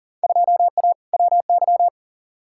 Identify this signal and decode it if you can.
Morse code, heard as a keyed tone, about 40 wpm, 700 Hz. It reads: 2A WY
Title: Field Day Exchanges - 40wpm